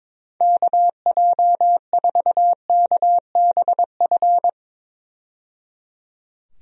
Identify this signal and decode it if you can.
Morse code, heard as a keyed tone, about 21 wpm, 700 Hz. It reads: KJ4KBF
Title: USA Call Signs - 6 Characters - 22wpm